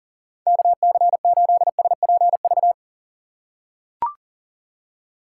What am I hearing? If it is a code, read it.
Morse code, heard as a keyed tone, about 37 wpm, 700 Hz. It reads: KC8SPV
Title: USA Call Signs - 6x Character Spacing - 40wpm